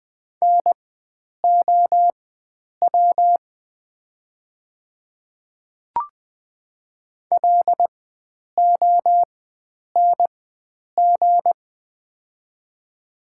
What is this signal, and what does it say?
Morse code, heard as a keyed tone, about 20 wpm, spaced as for 9 wpm, 700 Hz. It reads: NOW LONG
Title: 4x Character Spacing - Top 100 Words - 20wpm